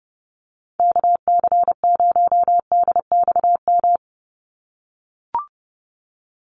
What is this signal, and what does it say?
Morse code, heard as a keyed tone, about 30 wpm, 700 Hz. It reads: KC0DXM
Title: USA Call Signs 30wpm